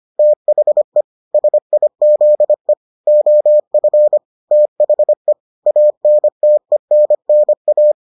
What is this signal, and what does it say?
Morse code, heard as a keyed tone, about 25 wpm, 600 Hz. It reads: THE SIZE OF THE ANTENNA